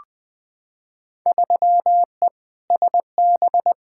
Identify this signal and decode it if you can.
Morse code, heard as a keyed tone, about 21 wpm, 700 Hz. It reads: 3E SB
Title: Field Day Exchanges Encore - 20wpm